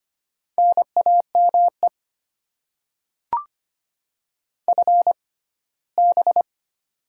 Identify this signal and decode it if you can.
Morse code, heard as a keyed tone, about 25 wpm, spaced as for 17 wpm, 690 Hz. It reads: NAME FB
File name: Top 20 QSO Elements - 6x Character Spacing - 25wpm